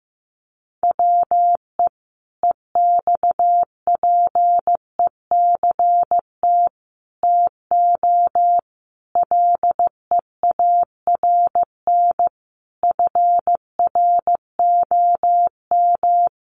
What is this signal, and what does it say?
Morse code, heard as a keyed tone, about 15 wpm, 700 Hz. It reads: WE EXPECT TO LEARN FROM
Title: Sentences from Top 1100 Words 15wpm